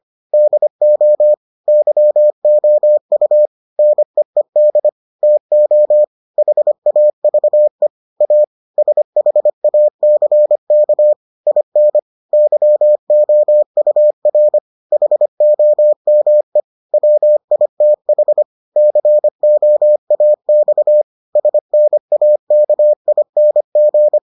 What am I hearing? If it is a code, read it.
Morse code, heard as a keyed tone, about 25 wpm, 600 Hz. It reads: DO YOU NEED TO HAVE A SHACK IN YOUR HOME WITH COAX SNAKING